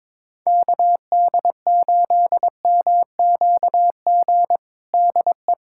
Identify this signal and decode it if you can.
Morse code, heard as a keyed tone, about 22 wpm, 700 Hz. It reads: KD8MQG DE
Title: USA Callsign DE Callsign 22wpm